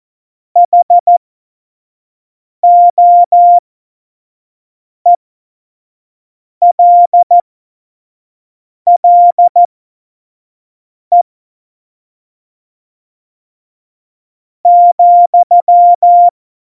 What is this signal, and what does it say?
Morse code, heard as a keyed tone, about 14 wpm, spaced as for 4 wpm, 700 Hz. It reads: HOELLE,
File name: Morsekurs / Witze / Witz 04. September 2018, 25 F70